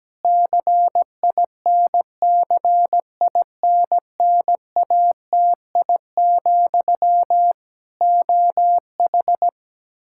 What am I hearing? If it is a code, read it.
Morse code, heard as a keyed tone, about 17 wpm, 700 Hz. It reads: CINCINNATI, OH
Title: Practice QSO - Exchange 1 - Beginning - 17wpm